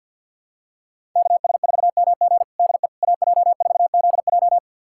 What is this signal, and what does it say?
Morse code, heard as a keyed tone, about 47 wpm, 700 Hz. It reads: KS4MG DE AJ4ZJ
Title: USA Callsign DE Callsign - 5 Characters - 50wpm